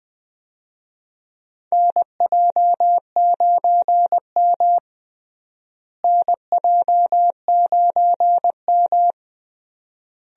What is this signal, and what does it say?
Morse code, heard as a keyed tone, about 21 wpm, 700 Hz. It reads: NJ9M NJ9M
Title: USA Call Signs - 4 Characters - Repeated 3x - 20wpm